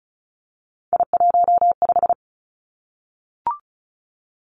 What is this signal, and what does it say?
Morse code, heard as a keyed tone, about 36 wpm, 700 Hz. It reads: I15
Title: Three Challenging Letter-Number Combinations 35wpm